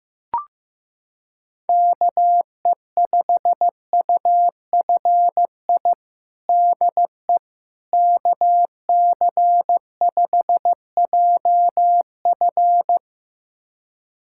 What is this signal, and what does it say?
Morse code, heard as a keyed tone, about 15 wpm, 700 Hz. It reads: KE5UFI DE KC5JF
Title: USA Callsign DE Callsign 15wpm